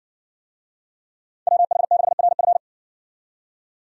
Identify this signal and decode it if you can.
Morse code, heard as a keyed tone, about 60 wpm, 700 Hz. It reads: WH6RF